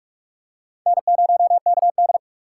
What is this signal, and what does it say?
Morse code, heard as a keyed tone, about 47 wpm, 700 Hz. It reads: N0KD